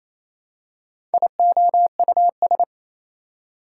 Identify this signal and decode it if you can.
Morse code, heard as a keyed tone, about 28 wpm, 700 Hz. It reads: IOUS
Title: Prefixes and Suffixes 28wpm